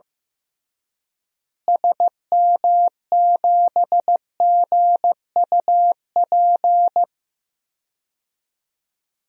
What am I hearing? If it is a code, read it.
Morse code, heard as a keyed tone, about 16 wpm, 700 Hz. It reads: SM7GUP
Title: DXCC-100 Call Signs - 6 Characters - Repeated 2x - 15wpm